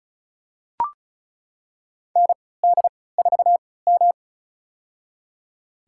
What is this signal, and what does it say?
Morse code, heard as a keyed tone, about 34 wpm, spaced as for 20 wpm, 700 Hz. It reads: ND4M
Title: USA Call Signs - 3x Character Spacing - 35wpm